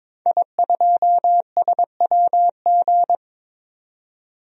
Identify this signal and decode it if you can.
Morse code, heard as a keyed tone, about 22 wpm, 700 Hz. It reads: I2SWG
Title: DXCC-100 Call Signs - 5 Characters - 22wpm